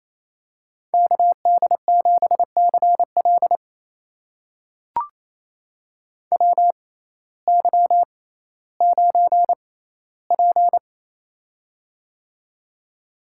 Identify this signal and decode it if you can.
Morse code, heard as a keyed tone, about 28 wpm, spaced as for 15 wpm, 700 Hz. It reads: KD7CL WY9P
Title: USA Call Signs - 6x Character Spacing - 28wpm